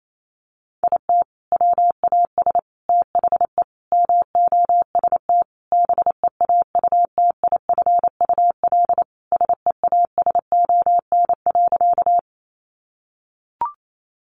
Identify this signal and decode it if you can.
Morse code, heard as a keyed tone, about 28 wpm, 700 Hz. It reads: IT WAS THE MOST BEAUTIFUL SEASON.
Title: Sentences from Top 1400 Words 28wpm